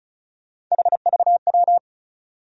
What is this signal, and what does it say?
Morse code, heard as a keyed tone, about 36 wpm, 700 Hz. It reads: HVW
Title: USA Call Sign Postfixes - 3 Letters 35wpm